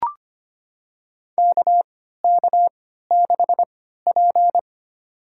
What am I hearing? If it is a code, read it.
Morse code, heard as a keyed tone, about 25 wpm, spaced as for 14 wpm, 700 Hz. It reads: KK6P
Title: USA Call Signs - 4 Characters - 3x Character Spacing - 25wpm